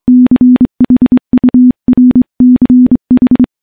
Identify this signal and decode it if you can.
Morse code, heard as a keyed tone, about 25 wpm, 250 Hz. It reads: CHURCH
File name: morse-training-25WPM